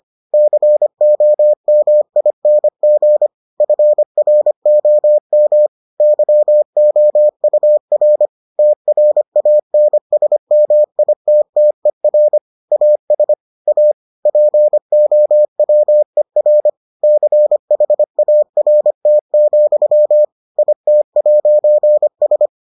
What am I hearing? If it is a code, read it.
Morse code, heard as a keyed tone, about 25 wpm, 600 Hz. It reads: COMING FROM YOUR TRANSMITTER AS A POWER CHART, IT'S